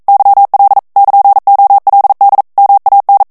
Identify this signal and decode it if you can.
Morse code, heard as a keyed tone, about 42 wpm, 800 Hz. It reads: YL <KN>OLD MAN
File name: morse mp3 / ITALIEN texte clair 1 vitesse 40 mots minute